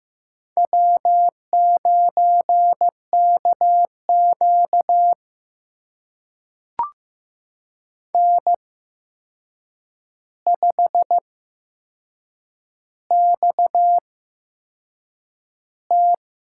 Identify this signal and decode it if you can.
Morse code, heard as a keyed tone, about 15 wpm, spaced as for 6 wpm, 700 Hz. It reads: W9KQ N5XT
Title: USA Call Signs - 8x Character Spacing - 15wpm